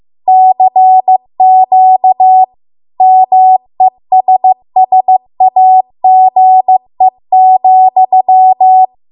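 Morse code, heard as a keyed tone, about 15 wpm, 750 Hz. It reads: CQ MESSAGE,
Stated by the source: Morse code practice at 15 wpm from the national amateur radio association's code archive.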